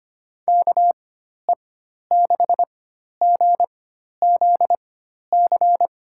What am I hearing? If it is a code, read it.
Morse code, heard as a keyed tone, about 25 wpm, spaced as for 12 wpm, 700 Hz. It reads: KE6GZC
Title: USA Call Signs - 6 Characters - 4x Character Spacing - 25wpm